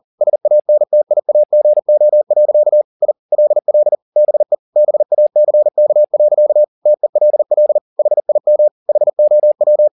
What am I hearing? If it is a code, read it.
Morse code, heard as a keyed tone, about 40 wpm, 600 Hz. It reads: SANTIAGO. I LL BE BACK. TELL HIM HOW
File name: the old man and the sea 2 40wpm